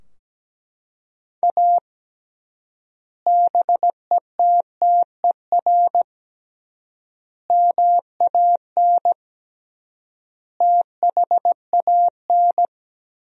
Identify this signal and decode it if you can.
Morse code, heard as a keyed tone, about 17 wpm, 700 Hz. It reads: A BETTER MAN THAN
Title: 3x Word-Spacing - Sentences from Top 400 Words - 17wpm